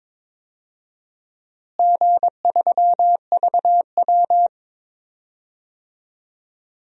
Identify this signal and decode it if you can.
Morse code, heard as a keyed tone, about 22 wpm, 700 Hz. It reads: G3VW
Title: DXCC-100 Call Signs - 4 Characters - Repeated 2x - 22wpm